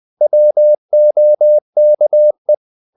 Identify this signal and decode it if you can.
Morse code, heard as a keyed tone, about 20 wpm, 600 Hz. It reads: WOKE
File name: the old man and the sea / the old man and the sea part 1 20wpm